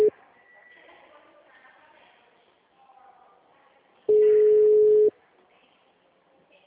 Ringback tone (Europe, 425 Hz).